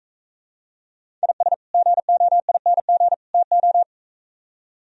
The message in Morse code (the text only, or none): IS GOING TO